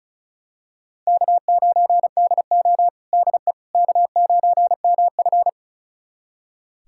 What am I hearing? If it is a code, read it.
Morse code, heard as a keyed tone, about 36 wpm, 700 Hz. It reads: K9DO DE K9MF
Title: USA Callsign DE Callsign - 4 Characters - 35wpm